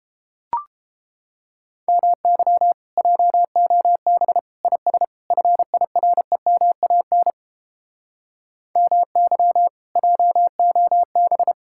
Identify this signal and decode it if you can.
Morse code, heard as a keyed tone, about 29 wpm, 700 Hz. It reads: MY JOB IS FIREMAN MY JOB